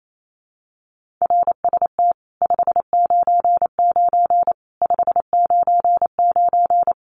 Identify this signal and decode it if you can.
Morse code, heard as a keyed tone, about 28 wpm, 700 Hz. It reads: RST 599 599
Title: Common QSO Phrases 28wpm